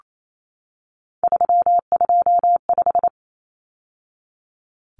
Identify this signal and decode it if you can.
Morse code, heard as a keyed tone, about 28 wpm, 700 Hz. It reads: 325
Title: Numbers - 3-Digit 28wpm